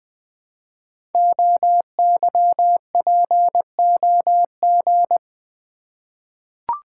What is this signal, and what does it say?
Morse code, heard as a keyed tone, about 20 wpm, 700 Hz. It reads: OYPOG
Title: Code Groups - 5 Letters - 20wpm